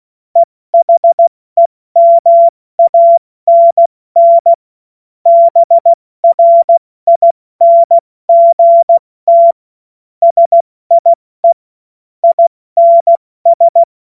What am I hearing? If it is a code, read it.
Morse code, heard as a keyed tone, about 16 wpm, 680 Hz. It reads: EHEMANN BRINGT SIE INS
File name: Morsekurs / Witze / Witz 27 Mai 2025 70 F80